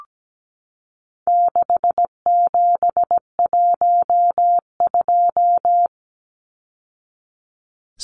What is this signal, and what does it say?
Morse code, heard as a keyed tone, about 17 wpm, 700 Hz. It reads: 6712
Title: Numbers - 4-Digit 17wpm